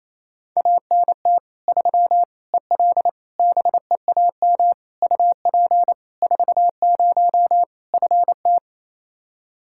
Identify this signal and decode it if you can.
Morse code, heard as a keyed tone, about 28 wpm, 700 Hz. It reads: ANT 3 EL BEAM UP 40 FT
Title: K1USN SST QSO Practice 28wpm